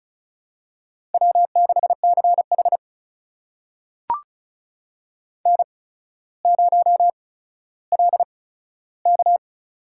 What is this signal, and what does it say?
Morse code, heard as a keyed tone, about 36 wpm, spaced as for 15 wpm, 700 Hz. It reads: W6CH N0LK
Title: USA Call Signs - 4 Characters - 8x Character Spacing - 35wpm